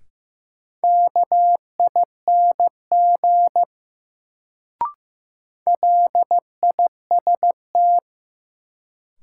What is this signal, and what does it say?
Morse code, heard as a keyed tone, about 15 wpm, 700 Hz. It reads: KING LIST
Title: Sentences from Top 400 Words - New Word Drill - 15wpm